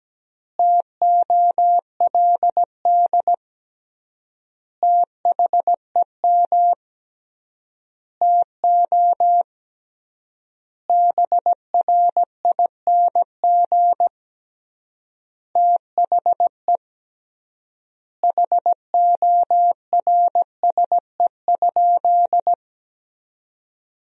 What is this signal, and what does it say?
Morse code, heard as a keyed tone, about 17 wpm, 700 Hz. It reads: TOLD THEM TO BRING THE HORSE?
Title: 3x Word-Spacing - Sentences from Top 500 Words - 17wpm